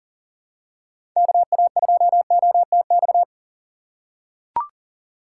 Morse code, heard as a keyed tone, about 39 wpm, 700 Hz. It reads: KA2OTX
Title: USA Call Signs - 2x Character Spacing - 40wpm